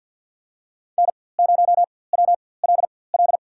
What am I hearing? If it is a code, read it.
Morse code, heard as a keyed tone, about 52 wpm, spaced as for 23 wpm, 700 Hz. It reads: N0WLL
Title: USA Call Signs - 5 Characters - 4x Character Spacing - 50wpm